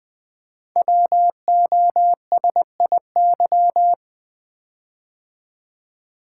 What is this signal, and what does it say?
Morse code, heard as a keyed tone, about 20 wpm, 700 Hz. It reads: WOSIY
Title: Code Groups - 5 Letter:Numbers - 20wpm